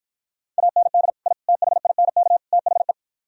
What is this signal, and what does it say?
Morse code, heard as a keyed tone, about 52 wpm, 700 Hz. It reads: AND I THINK THE